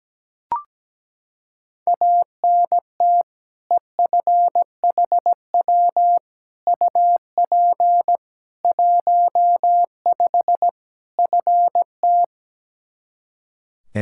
Morse code, heard as a keyed tone, about 17 wpm, 700 Hz. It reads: ANT EFHW UP 15 FT
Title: Practice QSO - Exchange 2 - Info - 17wpm